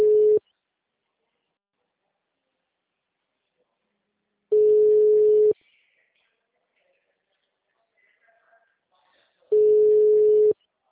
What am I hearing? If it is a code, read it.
Ringback tone (Europe, 425 Hz).
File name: out- -2013